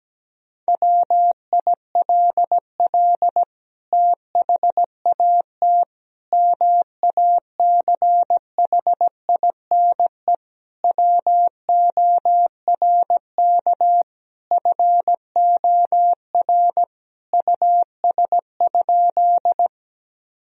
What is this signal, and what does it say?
Morse code, heard as a keyed tone, about 17 wpm, 700 Hz. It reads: WILL THAT MACHINE WORK FOR US?
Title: Sentences from Top 1000 Words - Review Part 2 - 17wpm